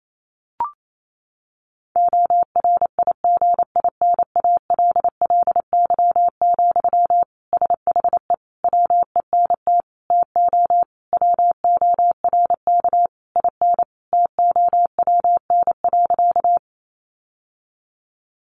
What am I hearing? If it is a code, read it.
Morse code, heard as a keyed tone, about 28 wpm, 700 Hz. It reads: ORIGINALLY, SHE WENT TO WORK IN TOWN.
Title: Sentences from Top 1500 Words 28wpm